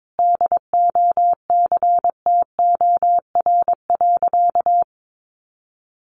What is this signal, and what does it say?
Morse code, heard as a keyed tone, about 22 wpm, 700 Hz. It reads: DOCTOR.
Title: Sentences from Top 1500 Words 22wpm